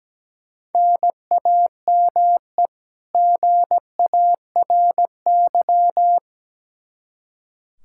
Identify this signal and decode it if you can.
Morse code, heard as a keyed tone, about 17 wpm, 700 Hz. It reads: NAME GARY
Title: Practice QSO - Exchange 1 - Beginning - 17wpm